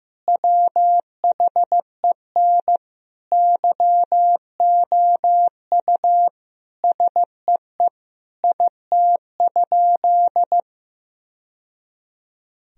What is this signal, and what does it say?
Morse code, heard as a keyed tone, about 15 wpm, 700 Hz. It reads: WHEN YOU SEE IT?
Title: Sentences from Top 1000 Words - Review Part 4 - 15wpm